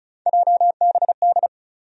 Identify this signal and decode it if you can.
Morse code, heard as a keyed tone, about 36 wpm, 700 Hz. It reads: JBD
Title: USA Call Sign Postfixes - 3 Letters 35wpm